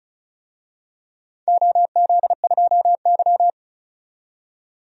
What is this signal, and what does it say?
Morse code, heard as a keyed tone, about 36 wpm, 700 Hz. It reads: OZ2Y
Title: DXCC-100 Call Signs - 4 Characters - 35wpm